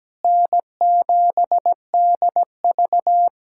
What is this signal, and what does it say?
Morse code, heard as a keyed tone, about 17 wpm, 700 Hz. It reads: N7DV
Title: USA Callsign DE Callsign - 4 Characters - 17wpm